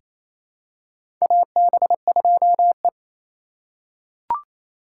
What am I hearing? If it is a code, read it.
Morse code, heard as a keyed tone, about 28 wpm, 700 Hz. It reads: AB2E
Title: USA Call Signs - 2x Character Spacing - 28wpm